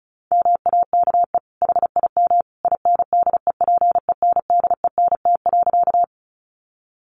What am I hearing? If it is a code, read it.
Morse code, heard as a keyed tone, about 34 wpm, 700 Hz. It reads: MAKE HIM INDEPENDENT.
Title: Sentences from Top 900 Words 35wpm